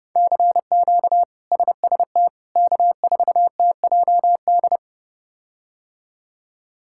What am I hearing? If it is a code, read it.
Morse code, heard as a keyed tone, about 31 wpm, 700 Hz. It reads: CQ SST K4TJD
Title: K1USN SST QSO Practice - 30wpm